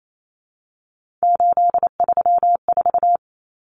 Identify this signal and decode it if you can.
Morse code, heard as a keyed tone, about 28 wpm, 700 Hz. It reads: 834
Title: Numbers - 3-Digit 28wpm